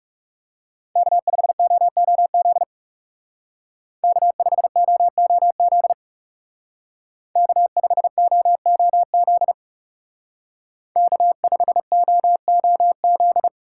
Morse code, heard as a keyed tone, about 39 wpm, 700 Hz. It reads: K5OOZ K5OOZ K5OOZ K5OOZ